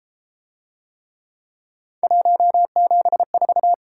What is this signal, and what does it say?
Morse code, heard as a keyed tone, about 31 wpm, 700 Hz. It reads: 174